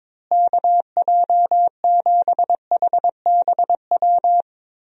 Morse code, heard as a keyed tone, about 22 wpm, 700 Hz. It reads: KJ7HBW
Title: USA Call Signs - 6 Characters - Repeated 3x - 22wpm